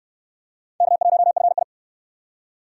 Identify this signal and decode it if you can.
Morse code, heard as a keyed tone, about 64 wpm, 700 Hz. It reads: K1FI